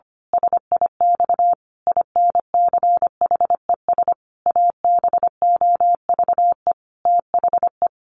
Morse code, heard as a keyed tone, about 25 wpm, 700 Hz. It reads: SIX INCHES ABOVE THE
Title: A Sound Of Thunder 25wpm